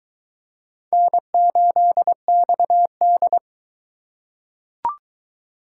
Morse code, heard as a keyed tone, about 22 wpm, 700 Hz. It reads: N8XD